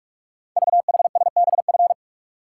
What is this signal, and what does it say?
Morse code, heard as a keyed tone, about 44 wpm, 690 Hz. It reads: VHSBF
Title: Code Groups - 5 Letters - 45wpm